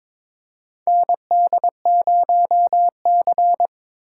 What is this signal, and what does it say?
Morse code, heard as a keyed tone, about 22 wpm, 700 Hz. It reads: ND0C
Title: USA Call Signs - 4 Characters - 2x Character Spacing - 22wpm